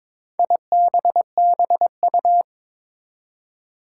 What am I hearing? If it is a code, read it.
Morse code, heard as a keyed tone, about 22 wpm, 700 Hz. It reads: IBBU
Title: Four Challenging Letter-Number Combinations 22wpm